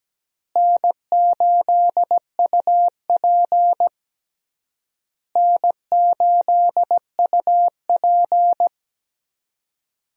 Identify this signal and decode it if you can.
Morse code, heard as a keyed tone, about 17 wpm, 700 Hz. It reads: N8UP N8UP
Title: USA Call Signs - 4 Characters - Repeated 3x - 17wpm